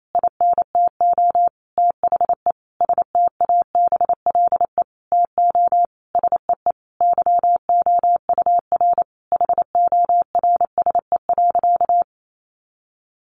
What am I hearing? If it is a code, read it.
Morse code, heard as a keyed tone, about 28 wpm, 700 Hz. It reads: INTO THE STABLE TO SEE YOUR HORSE.
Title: Sentences from Top 1500 Words 28wpm